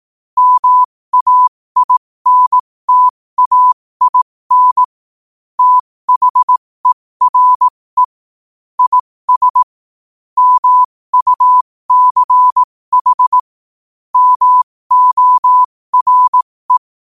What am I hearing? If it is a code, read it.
Morse code, heard as a keyed tone, about 18 wpm, spaced as for 15 wpm, 1000 Hz. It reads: MAINTAIN THERE IS MUCH MORE
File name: qotdfiles / 15wpm